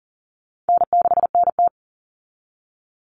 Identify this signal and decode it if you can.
Morse code, heard as a keyed tone, about 40 wpm, 700 Hz. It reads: N6NT
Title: Common Cut Numbers - 4-Digit 40wpm